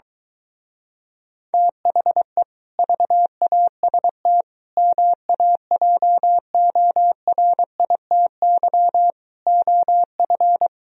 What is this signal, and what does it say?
Morse code, heard as a keyed tone, about 24 wpm, 700 Hz. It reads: THE VAST MAJORITY OF